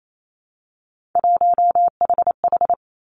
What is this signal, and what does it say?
Morse code, heard as a keyed tone, about 28 wpm, 700 Hz. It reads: 1HH
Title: Three Challenging Letter-Number Combinations 28wpm